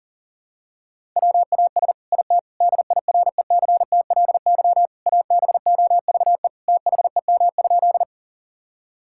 Morse code, heard as a keyed tone, about 40 wpm, 700 Hz. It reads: WAS IT DIRECTLY ABOVE THEM?
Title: Sentences from Top 1000 Words - Review Part 5 - 40wpm